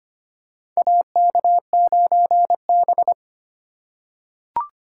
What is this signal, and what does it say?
Morse code, heard as a keyed tone, about 25 wpm, 700 Hz. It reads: AK9B
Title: USA Call Signs - 4 Characters - 4x Character Spacing - 25wpm